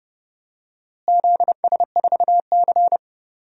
Morse code, heard as a keyed tone, about 30 wpm, 700 Hz. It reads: ZS4C
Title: DXCC-100 Call Signs - 4 Characters - Repeated 3x - 30wpm